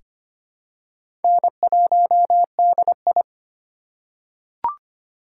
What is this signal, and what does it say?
Morse code, heard as a keyed tone, about 25 wpm, 700 Hz. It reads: N1DI
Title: USA Call Signs - 4 Characters - 3x Character Spacing - 25wpm